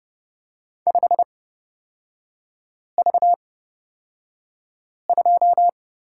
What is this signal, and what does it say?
Morse code, heard as a keyed tone, about 29 wpm, spaced as for 5 wpm, 690 Hz. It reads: 5V2
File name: Lesson 37e - Letter x - Three Characters 30wpm